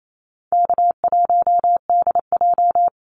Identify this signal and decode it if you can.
Morse code, heard as a keyed tone, about 28 wpm, 700 Hz. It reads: K1DJ
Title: Call Signs Repeated 3x - 28wpm